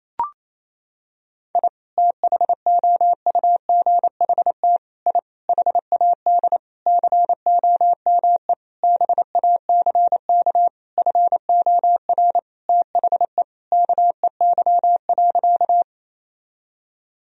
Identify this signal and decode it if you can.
Morse code, heard as a keyed tone, about 28 wpm, 700 Hz. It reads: I THOUGHT I HAD COME BACK FOR THE KEY.
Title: Sentences from Top 1000 Words - Review Part 4 - 28wpm